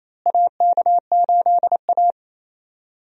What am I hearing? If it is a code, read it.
Morse code, heard as a keyed tone, about 28 wpm, 700 Hz. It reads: AK8A
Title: USA Call Signs - 4 Characters - Repeated 3x - 28wpm